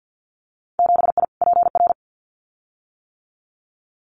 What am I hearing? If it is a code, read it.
Morse code, heard as a keyed tone, about 50 wpm, 700 Hz. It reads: 7I PR
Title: Winter Field Day Exchanges Encore 50wpm